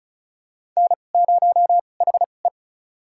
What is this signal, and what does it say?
Morse code, heard as a keyed tone, about 36 wpm, spaced as for 25 wpm, 700 Hz. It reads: N0HE
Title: USA Call Signs - 2x Character Spacing - 35wpm